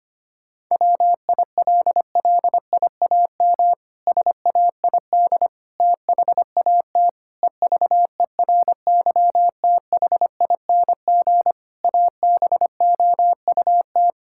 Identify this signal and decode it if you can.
Morse code, heard as a keyed tone, about 25 wpm, 700 Hz. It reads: WILLIAM SAID THAT EVERYTHING ABOUT
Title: Pangram Challenge v2 - 25wpm